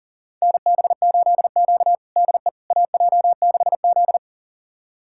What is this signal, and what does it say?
Morse code, heard as a keyed tone, about 40 wpm, 700 Hz. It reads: NB8Q DE AJ6Z
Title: USA Callsign DE Callsign - 4 Characters - 40wpm